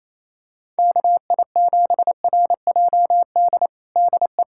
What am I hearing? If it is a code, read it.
Morse code, heard as a keyed tone, about 28 wpm, 700 Hz. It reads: KI7RJD DE
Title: USA Callsign DE Callsign - 6 Characters - 28wpm